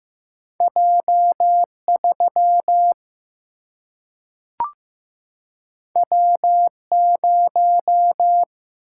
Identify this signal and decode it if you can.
Morse code, heard as a keyed tone, about 15 wpm, 700 Hz. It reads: J3 W0
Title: Single-Letter Single-Number 15wpm